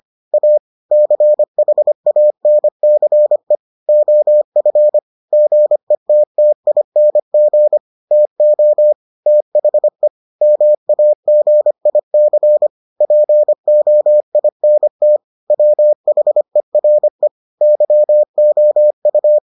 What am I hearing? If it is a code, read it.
Morse code, heard as a keyed tone, about 25 wpm, 600 Hz. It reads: A CHANCE OF GETTING TO THE MAGIC POINT WHERE YOU